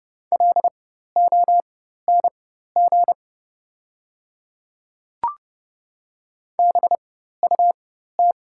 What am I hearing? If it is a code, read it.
Morse code, heard as a keyed tone, about 30 wpm, spaced as for 14 wpm, 700 Hz. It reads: LONG BUT
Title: 4x Character Spacing - Top 500 Words - 30wpm